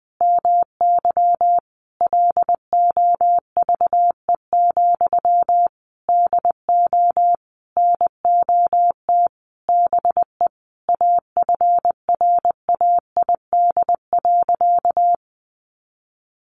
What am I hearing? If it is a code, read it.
Morse code, heard as a keyed tone, about 19 wpm, 700 Hz. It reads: MY LOVE, DO NOT BE AFRAID.
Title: Sentences from Top 1500 Words 20wpm